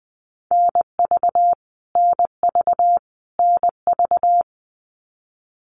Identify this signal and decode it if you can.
Morse code, heard as a keyed tone, about 20 wpm, 700 Hz. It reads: NV NV NV
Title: US States Repeated 3x 20wpm